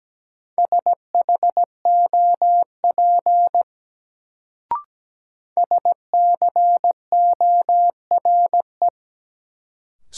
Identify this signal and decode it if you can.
Morse code, heard as a keyed tone, about 17 wpm, 700 Hz. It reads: SHOP SCORE
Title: Sentences from Top 1600 Words - New Word Drill - 17wpm